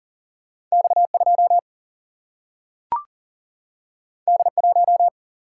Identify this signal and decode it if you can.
Morse code, heard as a keyed tone, about 40 wpm, 700 Hz. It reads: X2 D1
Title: Single-Letter Single-Number 40wpm